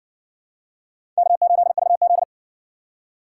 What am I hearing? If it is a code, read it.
Morse code, heard as a keyed tone, about 58 wpm, 700 Hz. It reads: K8VZ